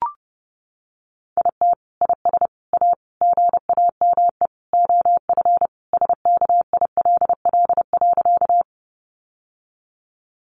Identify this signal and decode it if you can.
Morse code, heard as a keyed tone, about 28 wpm, 700 Hz. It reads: IT IS A GAME OF SKILL.
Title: Sentences from Top 1200 Words 30wpm